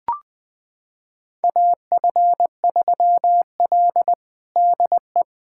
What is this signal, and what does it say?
Morse code, heard as a keyed tone, about 20 wpm, 700 Hz. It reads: AF3L DE
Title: USA Callsign DE Callsign - 4 Characters - 20wpm